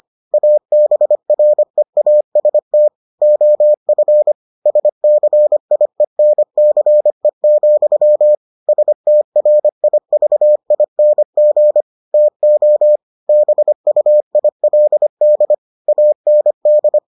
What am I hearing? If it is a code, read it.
Morse code, heard as a keyed tone, about 25 wpm, 600 Hz. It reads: ABREAST OF SCIENCE, STRIVING TO BUILD AND